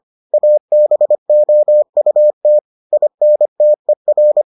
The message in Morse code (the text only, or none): ABOUT INTER